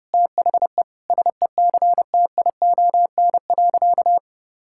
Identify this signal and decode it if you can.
Morse code, heard as a keyed tone, about 30 wpm, 700 Hz. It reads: THE SECTION.
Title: Sentences from Top 1000 Words - Review Part 5 - 30wpm